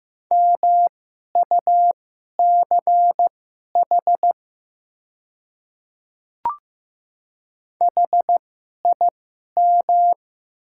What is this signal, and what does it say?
Morse code, heard as a keyed tone, about 15 wpm, spaced as for 11 wpm, 700 Hz. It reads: MUCH HIM
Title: 2x Character Spacing - Top 200 Words - 15wpm